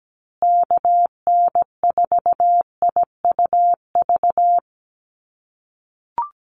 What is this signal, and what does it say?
Morse code, heard as a keyed tone, about 17 wpm, 700 Hz. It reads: KN4IUV
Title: USA Call Signs Repeated 3x 17wpm